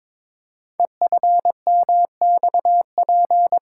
Morse code, heard as a keyed tone, about 22 wpm, 700 Hz. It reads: EFMXP
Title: Code Groups - 5 Letters - 22wpm